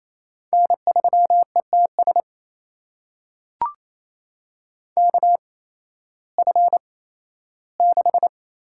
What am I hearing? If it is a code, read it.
Morse code, heard as a keyed tone, about 28 wpm, spaced as for 15 wpm, 700 Hz. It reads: N3ETS KF6
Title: USA Call Signs - 5 Characters - 8x Character Spacing - 28wpm